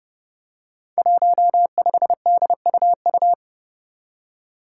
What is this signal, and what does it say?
Morse code, heard as a keyed tone, about 30 wpm, 700 Hz. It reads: 15DUU
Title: Five Challenging Letter-Number Combinations 30wpm